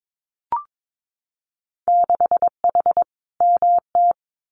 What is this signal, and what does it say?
Morse code, heard as a keyed tone, about 22 wpm, 700 Hz. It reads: 6H MT
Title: Winter Field Day Exchanges Encore 22wpm